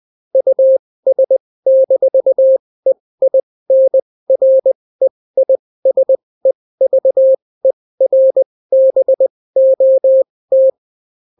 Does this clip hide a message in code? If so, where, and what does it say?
Yes: Morse code from 0.4 to 10.7 s, heard as a keyed tone, about 20 wpm, spaced as for 16 wpm, 530 Hz. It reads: US-EINREISEVERBOT